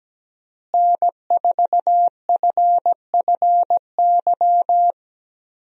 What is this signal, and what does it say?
Morse code, heard as a keyed tone, about 17 wpm, 700 Hz. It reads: N4FFY
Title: USA Call Signs - 5 Characters - Repeated 3x - 17wpm